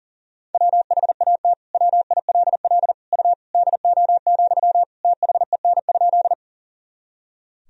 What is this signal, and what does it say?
Morse code, heard as a keyed tone, about 40 wpm, 700 Hz. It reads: WHAT WILL U DO, THEN?
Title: Sentences from Top 100 Words and Abbreviations 40wpm